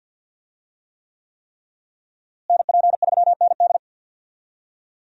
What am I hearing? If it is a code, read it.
Morse code, heard as a keyed tone, about 50 wpm, 700 Hz. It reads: NP3ND
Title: USA Call Signs - 5 Characters - Repeated 3x - 50wpm